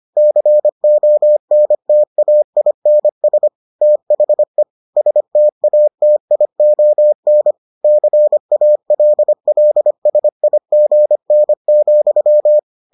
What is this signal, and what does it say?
Morse code, heard as a keyed tone, about 25 wpm, 600 Hz. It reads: CONTAINS THE STATION CALLSIGN,